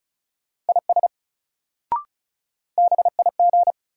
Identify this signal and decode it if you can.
Morse code, heard as a keyed tone, about 34 wpm, 700 Hz. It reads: IS BIG
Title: Sets of 2 Words - New Word Drill - 35wpm